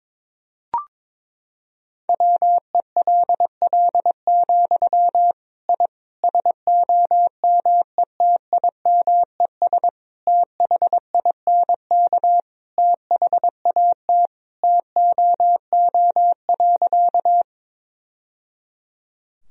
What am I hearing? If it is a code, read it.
Morse code, heard as a keyed tone, about 21 wpm, 700 Hz. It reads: WELL, I SOMETIMES THINK THAT TOO.
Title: Sentences from Top 1000 Words - Review Part 3 - 22wpm